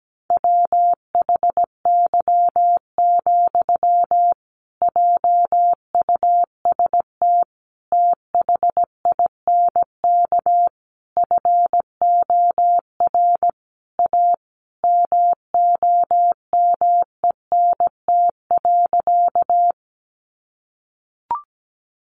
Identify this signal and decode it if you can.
Morse code, heard as a keyed tone, about 17 wpm, 700 Hz. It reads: WHY, JUST THINK FOR A MOMENT.
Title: Sentences from Top 1000 Words 17wpm